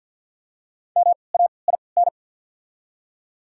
Morse code, heard as a keyed tone, about 53 wpm, spaced as for 29 wpm, 700 Hz. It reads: MAIN
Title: 3x Character Spacing - Top 300 Words - 50wpm